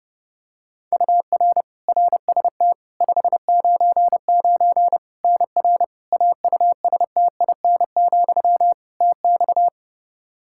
Morse code, heard as a keyed tone, about 30 wpm, 700 Hz. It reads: UR RST 599 NR AUSTIN, TX
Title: Practice QSO - Exchange 1 - Beginning - 30wpm